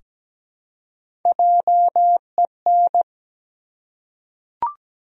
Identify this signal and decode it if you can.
Morse code, heard as a keyed tone, about 17 wpm, 700 Hz. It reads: JEN
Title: USA Common Names - 17wpm